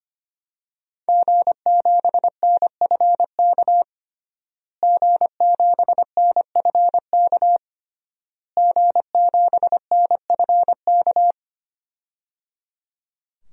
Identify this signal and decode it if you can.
Morse code, heard as a keyed tone, about 25 wpm, 700 Hz. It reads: G7NFK G7NFK G7NFK
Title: DXCC-100 Call Signs - 5 Characters - Repeated 3x - 25wpm